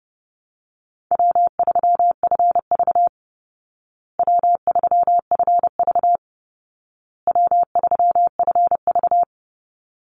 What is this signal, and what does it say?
Morse code, heard as a keyed tone, about 30 wpm, 700 Hz. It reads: W3FV W3FV W3FV
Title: Easy Call Signs Repeated 3x - 30wpm